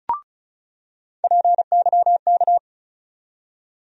Morse code, heard as a keyed tone, about 34 wpm, 700 Hz. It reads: PYK
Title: USA Call Sign Postfixes - 3 Letters 35wpm